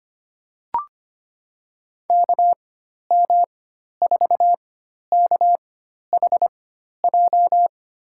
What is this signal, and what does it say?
Morse code, heard as a keyed tone, about 25 wpm, spaced as for 12 wpm, 700 Hz. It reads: KM4KHJ
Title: USA Call Signs - 4x Character Spacing - 25wpm